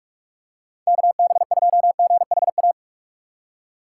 Morse code, heard as a keyed tone, about 42 wpm, 700 Hz. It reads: KB2GHA
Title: USA Call Signs - 6 Characters - 4x Character Spacing - 45wpm